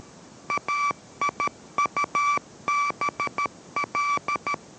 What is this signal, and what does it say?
Morse code, heard as a keyed tone, about 13 wpm, 1160 Hz. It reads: AIUBL